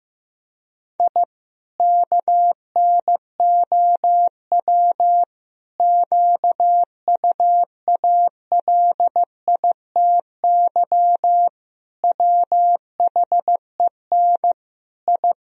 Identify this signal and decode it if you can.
Morse code, heard as a keyed tone, about 16 wpm, 700 Hz. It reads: I KNOW QUALITY WHEN I
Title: Sentences from Top 1000 Words - Review Part 4 - 15wpm